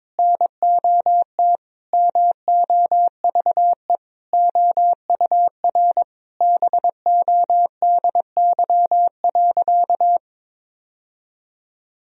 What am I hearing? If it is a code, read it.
Morse code, heard as a keyed tone, about 22 wpm, 700 Hz. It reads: NOT MOVE OUR BODY.
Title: Sentences from Top 1000 Words - Review Part 2 - 22wpm